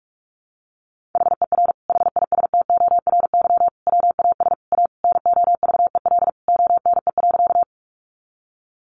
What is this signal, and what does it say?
Morse code, heard as a keyed tone, about 44 wpm, 700 Hz. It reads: HER HISTORY WAS A NOVEL ONE.
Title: Sentences from Top 1300 Words 45wpm